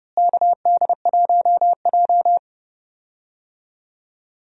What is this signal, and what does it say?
Morse code, heard as a keyed tone, about 29 wpm, 700 Hz. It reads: KD1J
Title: USA Call Signs - 4 Characters - Repeated 2x - 30wpm